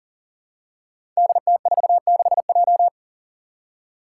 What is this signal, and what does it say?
Morse code, heard as a keyed tone, about 39 wpm, 700 Hz. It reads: DT46J
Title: Five Challenging Letter-Number Combinations 40wpm